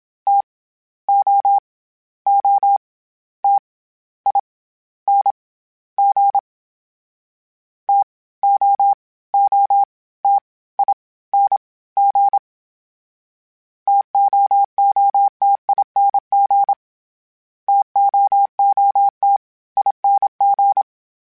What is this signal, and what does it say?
Morse code, heard as a keyed tone, about 27 wpm, 800 Hz. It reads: T O O T I N G T O O T I N G TOOTING TOOT ING
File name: Variable Speed / Variable Speed-mp3s / TO sp @25